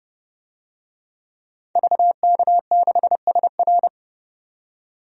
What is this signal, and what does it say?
Morse code, heard as a keyed tone, about 30 wpm, 700 Hz. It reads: VK6SR
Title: DXCC-100 Call Signs - 5 Characters - 30wpm